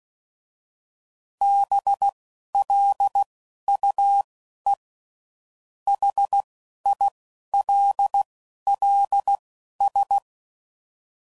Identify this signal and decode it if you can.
Morse code, heard as a keyed tone, about 16 wpm, spaced as for 12 wpm, 800 Hz. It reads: BLUE HILLS
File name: CW Code Groups @10WPM / fromSM's / Cities